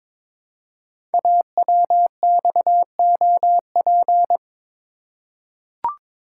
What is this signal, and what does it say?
Morse code, heard as a keyed tone, about 23 wpm, 700 Hz. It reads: AWXOP
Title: Code Groups - 5 Letters - 22wpm